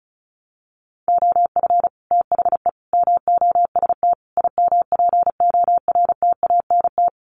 Morse code, heard as a keyed tone, about 34 wpm, 700 Hz. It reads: OF THE MOST IMPORTANT
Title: Sets of 4 Words 35wpm